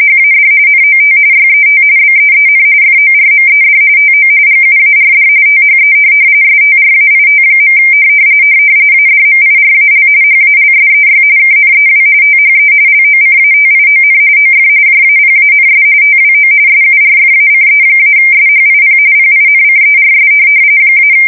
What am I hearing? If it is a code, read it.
RTTY, about 45.45 baud. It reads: HAD NOT YET BEEN FORMED TO GUARD THE STRAIT. ADMIRAL HALSEY KNEW THE LOCATION OF JAPANS TWO SURFACE-SHIP PINCERS -
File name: fsk170 / USS-New-Jersey